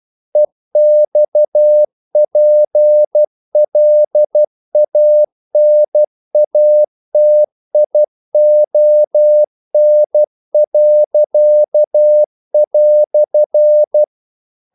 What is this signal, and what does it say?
Morse code, heard as a keyed tone, about 12 wpm, 600 Hz. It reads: EXPLANATION."
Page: English quotations in CW (telegraphy - morse)